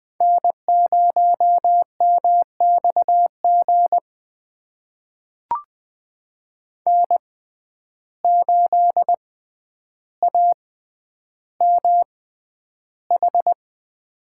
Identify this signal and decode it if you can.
Morse code, heard as a keyed tone, about 20 wpm, spaced as for 10 wpm, 700 Hz. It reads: N0MXG N8AMH
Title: USA Call Signs - 5 Characters - 6x Character Spacing - 20wpm